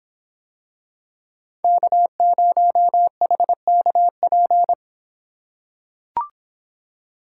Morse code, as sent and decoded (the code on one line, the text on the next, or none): -.- ----- .... -.- .--.
K0HKP